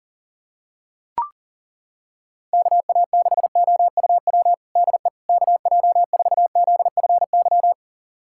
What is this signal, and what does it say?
Morse code, heard as a keyed tone, about 40 wpm, 700 Hz. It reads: KA6OUW DE KJ4ZFY
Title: USA Callsign DE Callsign - 6 Characters - 40wpm